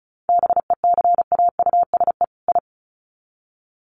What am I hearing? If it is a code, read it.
Morse code, heard as a keyed tone, about 34 wpm, 700 Hz. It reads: BECAUSE I
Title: Sets of 2 Words 35wpm